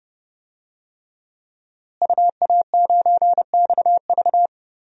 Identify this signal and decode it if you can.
Morse code, heard as a keyed tone, about 30 wpm, 700 Hz. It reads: UA9XV
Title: DXCC-100 Call Signs - 5 Characters - Repeated 3x - 30wpm